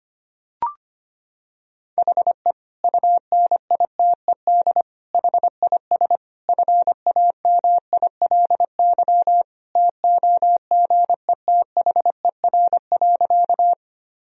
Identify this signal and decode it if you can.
Morse code, heard as a keyed tone, about 25 wpm, 700 Hz. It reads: HE UNITED HIS FAMILY TOGETHER.
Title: Sentences from Top 1000 Words - Review Part 4 - 25wpm